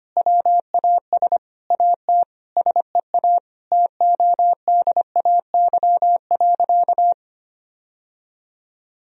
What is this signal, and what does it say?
Morse code, heard as a keyed tone, about 25 wpm, 700 Hz. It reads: WAS AT SEA TODAY.
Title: Sentences from Top 1000 Words - Review Part 2 - 25wpm